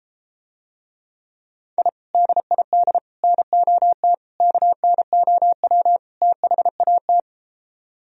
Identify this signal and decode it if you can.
Morse code, heard as a keyed tone, about 33 wpm, 700 Hz. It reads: I DID NOT KNOW THAT